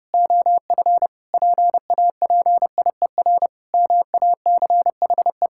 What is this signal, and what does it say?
Morse code, heard as a keyed tone, about 30 wpm, 700 Hz. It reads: OF PAPIER MACHE
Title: Pangram Challenge v2 - 30wpm